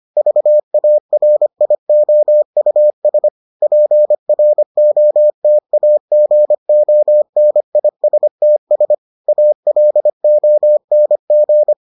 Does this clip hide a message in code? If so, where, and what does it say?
Yes: Morse code from 0.2 to 11.7 s, heard as a keyed tone, about 25 wpm, 600 Hz. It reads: VARIOUS PROTAGONISTS ALONG